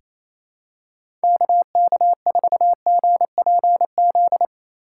Morse code, heard as a keyed tone, about 28 wpm, 700 Hz. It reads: KK4GPZ
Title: USA Call Signs - 6 Characters - 2x Character Spacing - 28wpm